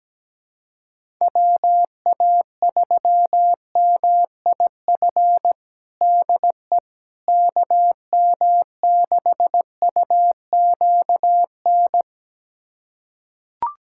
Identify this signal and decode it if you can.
Morse code, heard as a keyed tone, about 17 wpm, 700 Hz. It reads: WA3MIF DE KM6UQN
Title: USA Callsign DE Callsign - 6 Characters - 17wpm